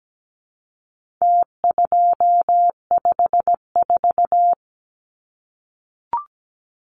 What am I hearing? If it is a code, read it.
Morse code, heard as a keyed tone, about 17 wpm, 700 Hz. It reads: T254
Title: Numbers - Contest Style - 4-Digit 17wpm